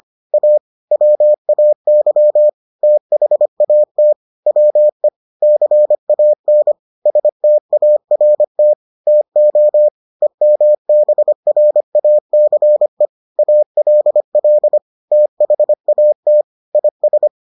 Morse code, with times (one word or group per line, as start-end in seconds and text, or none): 0.3-0.6 A
0.9-2.5 WAY
2.8-4.1 THAT
4.5-5.1 WE
5.4-6.7 CAN
7.1-8.7 START
9.1-9.9 TO
10.2-13.1 EMBRACE
13.4-14.8 ALL
15.1-16.4 THAT
16.7-17.3 IS